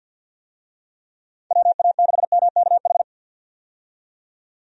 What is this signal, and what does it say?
Morse code, heard as a keyed tone, about 50 wpm, 700 Hz. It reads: WA6MKH
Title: USA Call Signs - 6 Characters - 50wpm